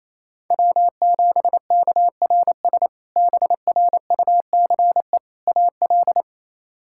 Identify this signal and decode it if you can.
Morse code, heard as a keyed tone, about 28 wpm, 700 Hz. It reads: W7KRS BRUCE AL
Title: K1USN SST QSO Practice - 28wpm